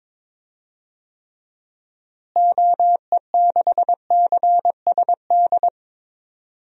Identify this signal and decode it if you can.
Morse code, heard as a keyed tone, about 22 wpm, 700 Hz. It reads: OE6CSD
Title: DXCC-100 Call Signs - 6 Characters - Repeated 2x - 22wpm